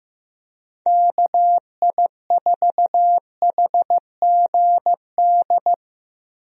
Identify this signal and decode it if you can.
Morse code, heard as a keyed tone, about 15 wpm, 700 Hz. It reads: KI4HGD
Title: USA Call Signs - 6 Characters - Repeated 2x - 15wpm